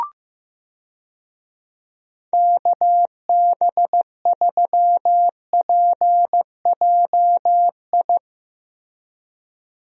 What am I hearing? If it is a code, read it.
Morse code, heard as a keyed tone, about 15 wpm, 700 Hz. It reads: KB3PJI
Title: USA Call Signs - 6 Characters - Repeated 2x - 15wpm